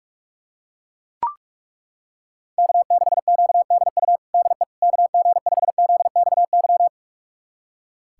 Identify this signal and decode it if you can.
Morse code, heard as a keyed tone, about 42 wpm, 700 Hz. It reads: K6QDU DE KG5ZXY
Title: USA Callsign DE Callsign 45wpm